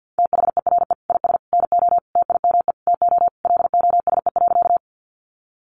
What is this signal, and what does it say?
Morse code, heard as a keyed tone, about 50 wpm, 700 Hz. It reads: THERE IS NO TIME TO LOSE.
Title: Sentences from Top 1100 Words 50wpm